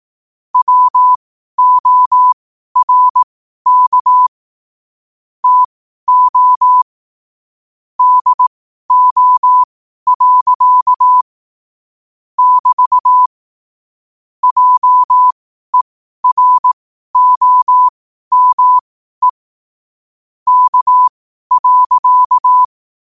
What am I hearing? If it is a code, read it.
Morse code, heard as a keyed tone, about 18 wpm, spaced as for 12 wpm, 1000 Hz. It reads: WORK TO DO. <BT> JEROME K.
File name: qotdfiles / 12wpm